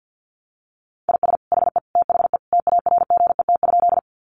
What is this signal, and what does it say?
Morse code, heard as a keyed tone, about 49 wpm, 700 Hz. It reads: IS HE THE TARGET?
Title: Sentences from Top 1200 Words 50wpm